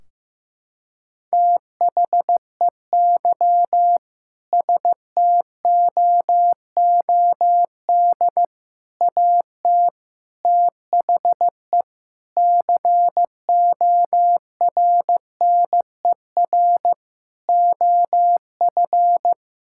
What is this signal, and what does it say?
Morse code, heard as a keyed tone, about 15 wpm, 700 Hz. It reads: THEY STOOD AT THE CORNER OF
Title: Sentences from Top 1600 Words 15wpm